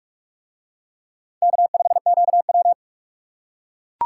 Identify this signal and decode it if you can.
Morse code, heard as a keyed tone, about 42 wpm, 700 Hz. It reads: K5QW